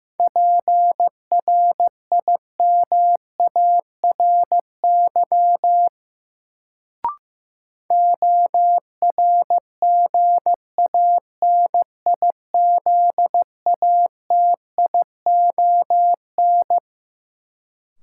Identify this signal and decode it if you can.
Morse code, heard as a keyed tone, about 15 wpm, 700 Hz. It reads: PRIMARY ORGANIZATION
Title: Sentences from Top 1000 Words - Review Part 5 - New Word Drill - 15wpm